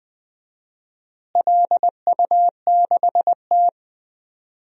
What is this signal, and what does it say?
Morse code, heard as a keyed tone, about 21 wpm, 700 Hz. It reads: LU6T
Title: DXCC-100 Call Signs - 4 Characters - Repeated 2x - 20wpm